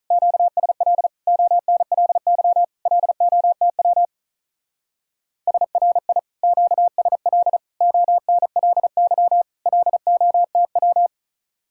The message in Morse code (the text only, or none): QSL ONLY LOTW SRI QSL ONLY LOTW